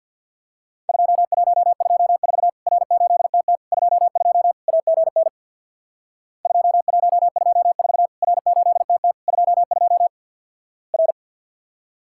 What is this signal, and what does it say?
Morse code, heard as a keyed tone, about 50 wpm, 700 Hz. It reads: 2124 R8TT 22 AGN 2124 R8TT 22 R
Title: Worked All Europe DX Contest - QTC Exchange with Repeats - 50wpm